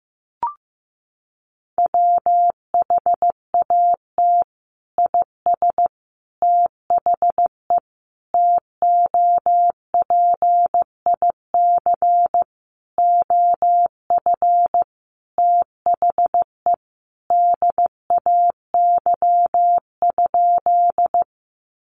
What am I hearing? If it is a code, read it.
Morse code, heard as a keyed tone, about 15 wpm, 700 Hz. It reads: WHAT IS THE TOPIC OF THE DAY?
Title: Sentences from Top 1200 Words 15wpm